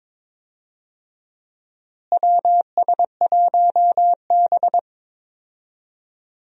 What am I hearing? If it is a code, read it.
Morse code, heard as a keyed tone, about 21 wpm, 700 Hz. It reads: WS1B
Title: USA Call Signs - 4 Characters - Repeated 2x - 22wpm